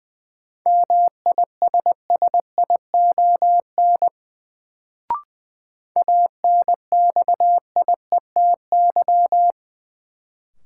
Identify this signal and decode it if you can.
Morse code, heard as a keyed tone, about 20 wpm, 700 Hz. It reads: MISSION ANXIETY
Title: Sentences from Top 1600 Words - New Word Drill - 20wpm